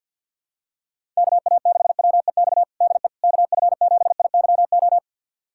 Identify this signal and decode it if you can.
Morse code, heard as a keyed tone, about 48 wpm, 700 Hz. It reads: KA6WEX DE KF7IYO
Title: USA Callsign DE Callsign - 6 Characters - 50wpm